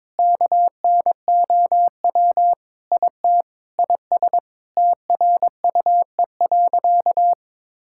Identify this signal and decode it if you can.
Morse code, heard as a keyed tone, about 22 wpm, 700 Hz. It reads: KNOW IT IS TRUE.
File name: practice / 22wpm / Sentences with 4-Word Phrases - 22wpm